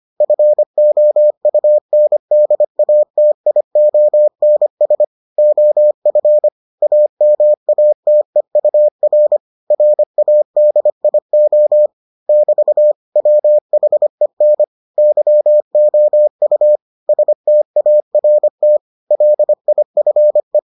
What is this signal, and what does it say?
Morse code, heard as a keyed tone, about 25 wpm, 600 Hz. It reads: FOUNDATIONS OF AMATEUR RADIO <BT> WHEN YOU START LIFE